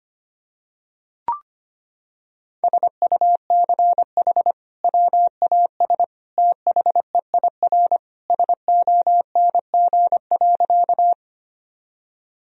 Morse code, heard as a keyed tone, about 25 wpm, 700 Hz. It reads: SUCH WAS THEIR SONG.
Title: Sentences from Top 1000 Words - Review Part 2 - 25wpm